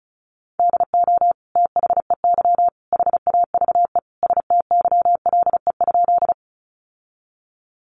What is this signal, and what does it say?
Morse code, heard as a keyed tone, about 34 wpm, 700 Hz. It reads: DO THEY HAVE STYLE?
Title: Sentences from Top 800 Words 35wpm